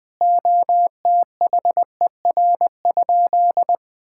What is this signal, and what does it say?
Morse code, heard as a keyed tone, about 20 wpm, 700 Hz. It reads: OTHER?
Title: Sentences from Top 1000 Words - Review Part 6 - 20wpm